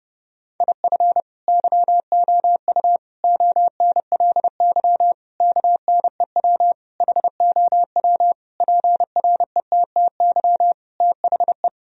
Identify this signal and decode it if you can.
Morse code, heard as a keyed tone, about 32 wpm, 700 Hz. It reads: IF YOU ONLY KNEW HOW PRETTY THE
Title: Sentences from Top 1000 Words - Review Part 4 - 30wpm